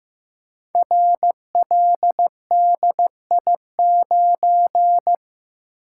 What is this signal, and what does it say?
Morse code, heard as a keyed tone, about 15 wpm, 700 Hz. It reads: RLDI9
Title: Code Groups - 5 Letter:Numbers - 15wpm